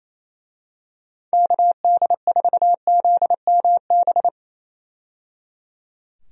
Morse code, heard as a keyed tone, about 28 wpm, 700 Hz. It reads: KD4ZMB
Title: USA Call Signs - 6 Characters - 28wpm